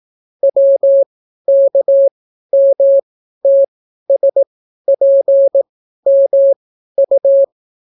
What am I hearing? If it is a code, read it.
Morse code, heard as a keyed tone, about 18 wpm, spaced as for 12 wpm, 550 Hz. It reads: WKMTSPMU